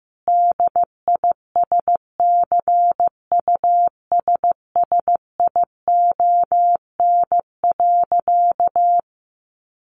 Morse code, heard as a keyed tone, about 15 wpm, 700 Hz. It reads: DISCUSSION.
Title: Sentences from Top 1100 Words 15wpm